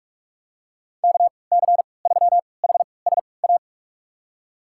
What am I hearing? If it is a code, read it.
Morse code, heard as a keyed tone, about 46 wpm, spaced as for 26 wpm, 700 Hz. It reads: KC3HSA
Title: USA Call Signs - 6 Characters - 3x Character Spacing - 45wpm